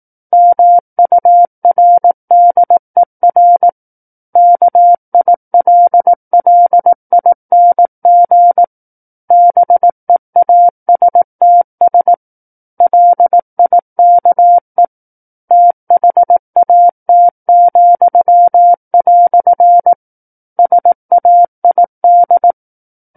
Morse code, heard as a keyed tone, about 18 wpm, 700 Hz. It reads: MURDER KILLING BEASTS LIKE THAT," SAID
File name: 18 WPM